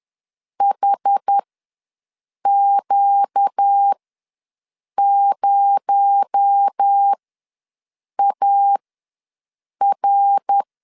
Morse code, heard as a keyed tone, about 10 wpm, spaced as for 6 wpm, 800 Hz. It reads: HQ0AR
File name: LEZIONI / 9 286 ws 10wpm 4032